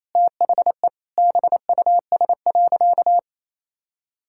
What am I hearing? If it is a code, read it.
Morse code, heard as a keyed tone, about 28 wpm, 700 Hz. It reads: THE BUS.
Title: Sentences from Top 1000 Words - Review Part 6 - 28wpm